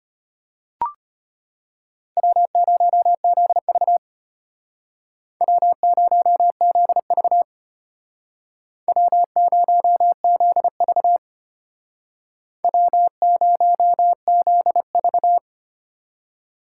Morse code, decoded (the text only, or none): W0ZV W0ZV W0ZV W0ZV